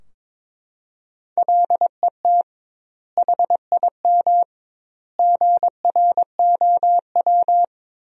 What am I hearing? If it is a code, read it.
Morse code, heard as a keyed tone, about 22 wpm, 700 Hz. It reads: LET HIM GROW
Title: 2x Word-Spacing - Sentences from Top 300 Words - 22wpm